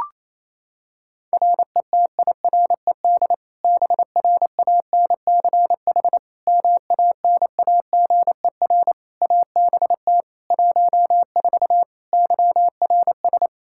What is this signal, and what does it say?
Morse code, heard as a keyed tone, about 28 wpm, 700 Hz. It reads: RETIRED BRANCH MANAGER ABT 14 YRS
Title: Practice QSO - Exchange 3 - More Info - 28wpm